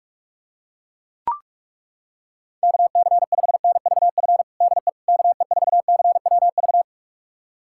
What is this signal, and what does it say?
Morse code, heard as a keyed tone, about 42 wpm, 700 Hz. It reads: KC5NVF DE KE4CWV
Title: USA Callsign DE Callsign 45wpm